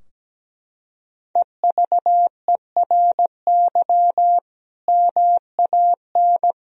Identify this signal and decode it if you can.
Morse code, heard as a keyed tone, about 17 wpm, 700 Hz. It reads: EVERY MAN
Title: Sentences from Top 1000 Words - Review Part 4 - 17wpm